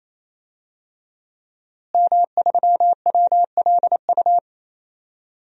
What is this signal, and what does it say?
Morse code, heard as a keyed tone, about 28 wpm, 700 Hz. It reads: M3WLU
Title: DXCC-100 Call Signs - 5 Characters - Repeated 3x - 28wpm